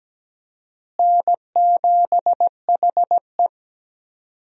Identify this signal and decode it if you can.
Morse code, heard as a keyed tone, about 17 wpm, 700 Hz. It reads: N7HE
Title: USA Call Signs - 4 Characters - Repeated 2x - 17wpm